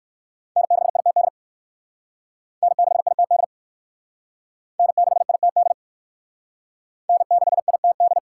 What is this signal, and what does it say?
Morse code, heard as a keyed tone, about 60 wpm, 700 Hz. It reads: N6ITD N6ITD N6ITD N6ITD